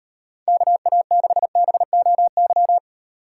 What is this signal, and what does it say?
Morse code, heard as a keyed tone, about 37 wpm, 700 Hz. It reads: KA6BOY